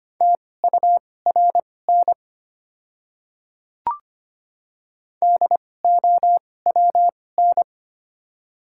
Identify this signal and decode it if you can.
Morse code, heard as a keyed tone, about 25 wpm, spaced as for 18 wpm, 700 Hz. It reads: TURN DOWN
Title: 2x Character Spacing - Top 300 Words - 25wpm